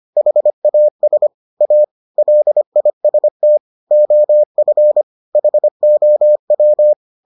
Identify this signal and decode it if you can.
Morse code, heard as a keyed tone, about 25 wpm, 600 Hz. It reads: HAS A LIST OF HOW